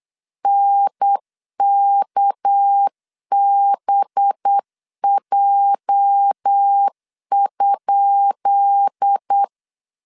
Morse code, heard as a keyed tone, about 8 wpm, 800 Hz. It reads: NKBJ?
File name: LEZIONI / 10 SIGNS / 10B INT 8wpm 3534